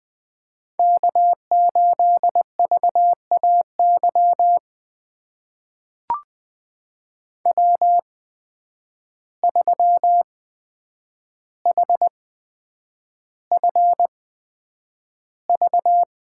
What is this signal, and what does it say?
Morse code, heard as a keyed tone, about 20 wpm, spaced as for 9 wpm, 700 Hz. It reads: K8VAY W3HFV
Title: USA Call Signs - 5 Characters - 8x Character Spacing - 20wpm